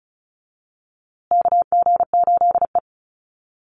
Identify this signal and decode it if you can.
Morse code, heard as a keyed tone, about 34 wpm, 700 Hz. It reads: KG8E
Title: USA Call Signs Repeated 3x 35wpm